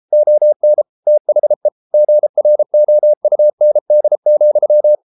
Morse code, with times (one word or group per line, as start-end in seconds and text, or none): 0.1-0.8 ON
1.1-1.7 THE
1.9-4.9 GROUND,